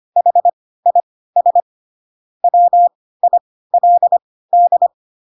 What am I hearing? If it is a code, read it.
Morse code, heard as a keyed tone, about 25 wpm, spaced as for 16 wpm, 700 Hz. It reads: HIS WILD
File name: Pinocchio / Pinocchio ch4 10 25 ik1hge